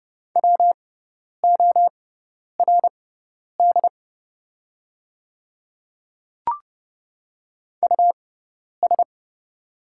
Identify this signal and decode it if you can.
Morse code, heard as a keyed tone, about 30 wpm, spaced as for 10 wpm, 700 Hz. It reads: WORD US
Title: 6x Character Spacing - Top 300 Words - 30wpm